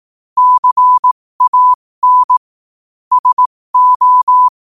Morse code, heard as a keyed tone, about 18 wpm, spaced as for 15 wpm, 1000 Hz. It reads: CAN SO
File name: qotdfiles / 15wpm